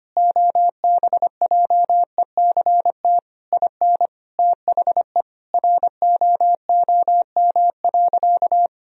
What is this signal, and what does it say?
Morse code, heard as a keyed tone, about 25 wpm, 700 Hz. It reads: OBJECT IN THE ROOM.
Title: Sentences from Top 1000 Words - Review Part 3 - 25wpm